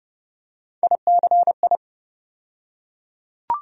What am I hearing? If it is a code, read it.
Morse code, heard as a keyed tone, about 31 wpm, 700 Hz. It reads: ICI
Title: USA Call Sign Postfixes - 3 Letters 30wpm